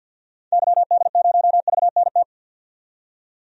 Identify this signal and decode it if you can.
Morse code, heard as a keyed tone, about 51 wpm, 700 Hz. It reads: YD0VNT
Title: DXCC-100 Call Signs - 6 Characters - 50wpm